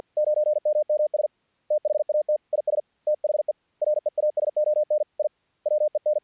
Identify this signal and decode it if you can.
Morse code, heard as a keyed tone, about 50 wpm, 610 Hz. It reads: 9MMS THAT IS THE REASON I WEN
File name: archived / CRSnet